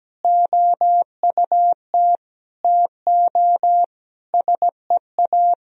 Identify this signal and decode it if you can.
Morse code, heard as a keyed tone, about 17 wpm, 700 Hz. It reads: OUT TO SEA
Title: Sentences from Top 1000 Words - Review Part 3 - 17wpm